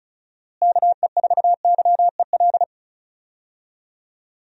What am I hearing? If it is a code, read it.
Morse code, heard as a keyed tone, about 36 wpm, 700 Hz. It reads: KE4YEL
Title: USA Call Signs - 6 Characters - Repeated 3x - 35wpm